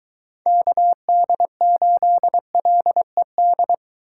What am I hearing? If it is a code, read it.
Morse code, heard as a keyed tone, about 23 wpm, 700 Hz. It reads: KD8LED